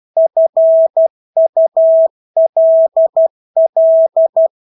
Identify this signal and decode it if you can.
Morse code, heard as a keyed tone, about 12 wpm, 650 Hz. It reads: FULL
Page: Engelske ord 12 wpm